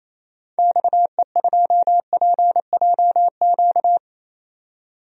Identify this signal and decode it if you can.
Morse code, heard as a keyed tone, about 28 wpm, 700 Hz. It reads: XE2PJQ
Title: DXCC-100 Call Signs - 6 Characters - 28wpm